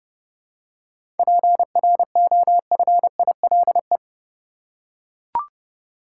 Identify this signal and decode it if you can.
Morse code, heard as a keyed tone, about 30 wpm, 700 Hz. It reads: PROFILE
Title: Seven Letter Words - 30wpm